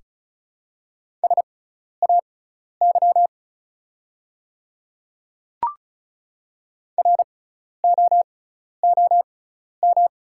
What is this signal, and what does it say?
Morse code, heard as a keyed tone, about 36 wpm, spaced as for 12 wpm, 690 Hz. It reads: SAY ROOM
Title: 6x Character Spacing - Top 300 Words - 35wpm